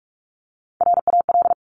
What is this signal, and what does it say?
Morse code, heard as a keyed tone, about 46 wpm, 690 Hz. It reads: RAL
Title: Top Three Letter Combinations in Words 45wpm